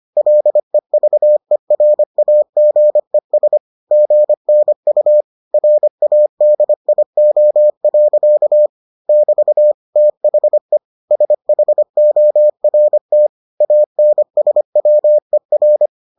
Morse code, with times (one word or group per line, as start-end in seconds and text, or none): 0.2-3.6 LEVERAGES
3.9-5.2 GNU
5.5-8.7 RADIO.
9.1-9.7 <BT>
10.0-10.8 THE
11.1-13.3 SHORT
13.6-15.9 ANSWER